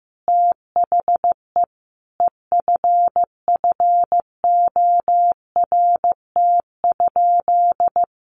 Morse code, heard as a keyed tone, about 15 wpm, 700 Hz. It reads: THE EFFORT?
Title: Sentences from Top 1200 Words 15wpm